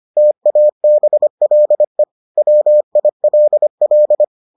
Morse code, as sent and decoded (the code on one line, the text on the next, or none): - .- -... .-.. / .-- .. .-.. .-..
TABLE WILL